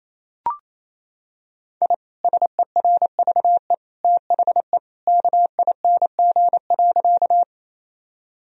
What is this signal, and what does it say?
Morse code, heard as a keyed tone, about 28 wpm, 700 Hz. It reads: I SERVE THE KING.
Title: Sentences from Top 1000 Words - Review Part 2 - 28wpm